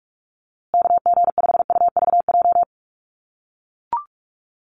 Morse code, heard as a keyed tone, about 44 wpm, 700 Hz. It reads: KG5UVJ
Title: USA Call Signs Repeated 3x 45wpm